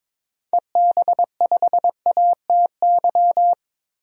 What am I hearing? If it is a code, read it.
Morse code, heard as a keyed tone, about 22 wpm, 700 Hz. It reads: EB5ATY
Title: DXCC-100 Call Signs - 6 Characters - 22wpm